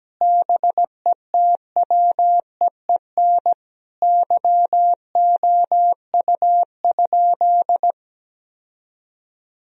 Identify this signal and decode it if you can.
Morse code, heard as a keyed tone, about 17 wpm, 700 Hz. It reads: BETWEEN YOU?
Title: Sentences from Top 1000 Words - Review Part 4 - 17wpm